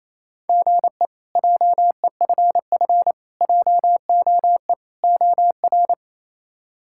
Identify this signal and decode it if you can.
Morse code, heard as a keyed tone, about 28 wpm, 700 Hz. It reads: GE JEFF JOE OR
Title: K1USN SST QSO Practice - 28wpm